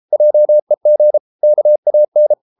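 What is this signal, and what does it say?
Morse code, heard as a keyed tone, about 33 wpm, 600 Hz. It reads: JEG KAN
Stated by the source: Morse code at 33 wpm, QRQ.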